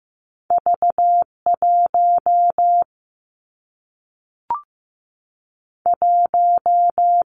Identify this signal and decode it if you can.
Morse code, heard as a keyed tone, about 15 wpm, 700 Hz. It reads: V 1 1
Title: Two Challenging Letter-Number Combinations 15wpm